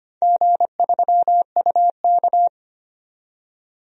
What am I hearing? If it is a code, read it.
Morse code, heard as a keyed tone, about 25 wpm, 700 Hz. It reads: G3UK
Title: DXCC-100 Call Signs - 4 Characters - 25wpm